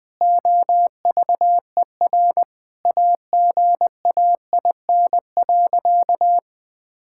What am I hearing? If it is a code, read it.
Morse code, heard as a keyed tone, about 20 wpm, 700 Hz. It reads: OVER AGAIN.
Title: Sentences with 4-Word Phrases - 20wpm